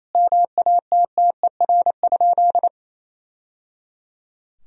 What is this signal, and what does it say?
Morse code, heard as a keyed tone, about 28 wpm, 700 Hz. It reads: MATTER?
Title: Sentences from Top 1000 Words - Review Part 4 - 28wpm